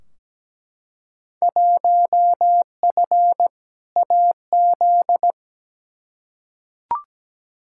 Morse code, heard as a keyed tone, about 17 wpm, 700 Hz. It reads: 1F AZ
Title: Field Day Exchanges - 17wpm